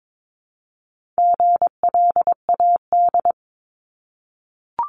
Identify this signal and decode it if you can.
Morse code, heard as a keyed tone, about 22 wpm, 700 Hz. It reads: GLAD
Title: Multi-letter Phonemes 22wpm